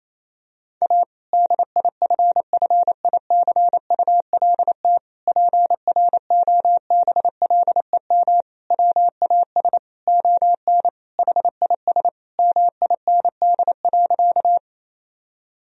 Morse code, heard as a keyed tone, about 28 wpm, 700 Hz. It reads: A DIFFICULT PROBLEM WAS ON HIS MIND.
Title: Sentences from Top 1000 Words - Review Part 2 - 28wpm